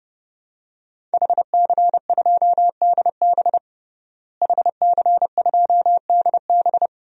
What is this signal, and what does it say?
Morse code, heard as a keyed tone, about 30 wpm, 700 Hz. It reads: HC2DB HC2DB
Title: DXCC-100 Call Signs - 5 Characters - Repeated 2x - 30wpm